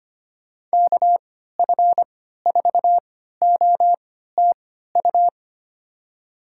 Morse code, heard as a keyed tone, about 25 wpm, spaced as for 14 wpm, 700 Hz. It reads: KF4OTU
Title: USA Call Signs - 6 Characters - 3x Character Spacing - 25wpm